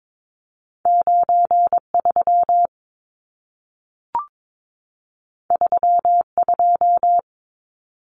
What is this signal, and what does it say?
Morse code, heard as a keyed tone, about 22 wpm, 700 Hz. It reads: 93 32
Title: Numbers - 2-Digit 22wpm